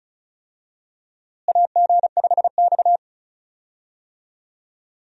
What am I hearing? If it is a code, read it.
Morse code, heard as a keyed tone, about 36 wpm, 700 Hz. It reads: AG5X
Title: USA Call Signs - 4 Characters - Repeated 3x - 35wpm